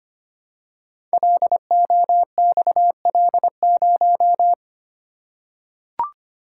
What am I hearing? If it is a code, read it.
Morse code, heard as a keyed tone, about 25 wpm, 700 Hz. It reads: LOXL0
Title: Code Groups - 5 Letter:Numbers - 25wpm